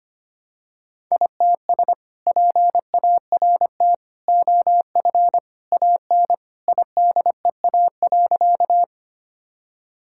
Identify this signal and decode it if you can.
Morse code, heard as a keyed tone, about 25 wpm, 700 Hz. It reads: ITS PART OF AN IDEA.
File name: practice / 25wpm / Sentences with 2-Word Phrases - 25wpm